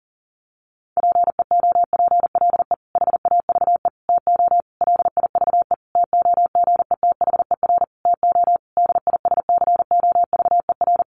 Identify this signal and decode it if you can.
Morse code, heard as a keyed tone, about 40 wpm, 700 Hz. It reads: PEOPLE HAVE TO LIVE TOGETHER TO DISCOVER
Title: Sentences from Top 1100 Words 40wpm